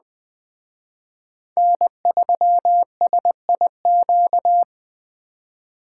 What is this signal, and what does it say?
Morse code, heard as a keyed tone, about 20 wpm, 700 Hz. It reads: N3SIQ
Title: Speed Racing - USA 1x3 Call Signs 17wpm